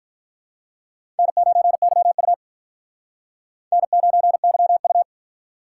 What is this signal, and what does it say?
Morse code, heard as a keyed tone, about 43 wpm, 700 Hz. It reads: N9YU N9YU
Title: Speed Racing - USA Call Signs 35wpm